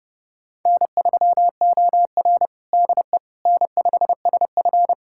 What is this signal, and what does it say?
Morse code, heard as a keyed tone, about 31 wpm, 700 Hz. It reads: N3OR DE N5SF
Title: USA Callsign DE Callsign - 4 Characters - 30wpm